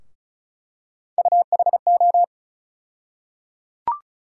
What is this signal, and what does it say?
Morse code, heard as a keyed tone, about 34 wpm, 700 Hz. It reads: UHO
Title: USA Call Sign Postfixes - 3 Letters 35wpm